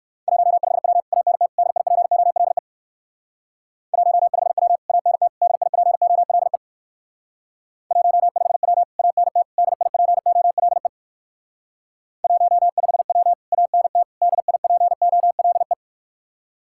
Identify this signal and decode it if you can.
Morse code, heard as a keyed tone, about 60 wpm, 700 Hz. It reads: 15W ANT DIPOLE 15W ANT DIPOLE 15W ANT DIPOLE 15W ANT DIPOLE